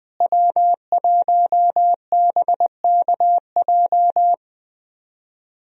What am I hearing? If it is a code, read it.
Morse code, heard as a keyed tone, about 21 wpm, 700 Hz. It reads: W1BKJ
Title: USA Call Signs - 5 Characters - Repeated 2x - 20wpm